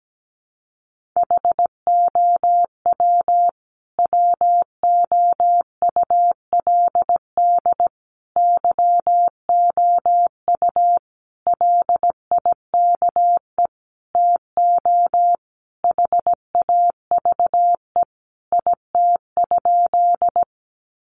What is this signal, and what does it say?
Morse code, heard as a keyed tone, about 17 wpm, 700 Hz. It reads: HOW WOULD YOU LIKE TO HAVE IT?
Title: Sentences from Top 100 Words 17wpm